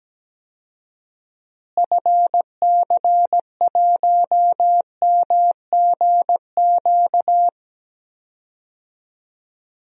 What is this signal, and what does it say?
Morse code, heard as a keyed tone, about 17 wpm, 700 Hz. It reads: FC1MGQ
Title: DXCC-100 Call Signs - 6 Characters - Repeated 2x - 17wpm